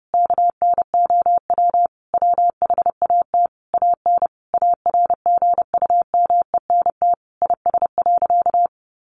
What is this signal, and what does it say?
Morse code, heard as a keyed tone, about 28 wpm, 700 Hz. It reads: KNOW WHAT AN ARGUMENT IS.
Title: Sentences from Top 1400 Words 30wpm